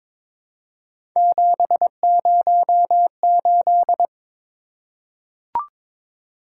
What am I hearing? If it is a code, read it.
Morse code, heard as a keyed tone, about 22 wpm, 700 Hz. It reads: 708
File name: practice / 22wpm / Three Letter-Number Combinations - 22wpm